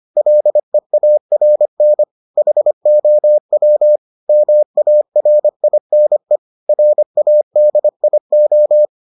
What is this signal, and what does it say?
Morse code, heard as a keyed tone, about 25 wpm, 600 Hz. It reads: LEARN HOW MARINE RADIO